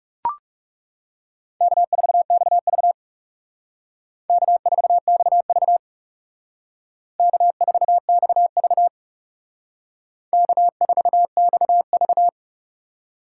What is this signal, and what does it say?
Morse code, heard as a keyed tone, about 38 wpm, 700 Hz. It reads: K4XV K4XV K4XV K4XV